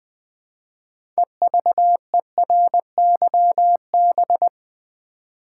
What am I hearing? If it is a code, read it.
Morse code, heard as a keyed tone, about 21 wpm, 700 Hz. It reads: EVERYB
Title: Build A Word - 3 Letters At A Time - 9-Letter Words - 20wpm